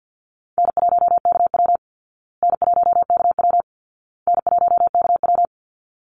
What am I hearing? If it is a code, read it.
Morse code, heard as a keyed tone, about 50 wpm, 700 Hz. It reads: N1KW N1KW N1KW
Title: Easy Call Signs Repeated 3x - 50wpm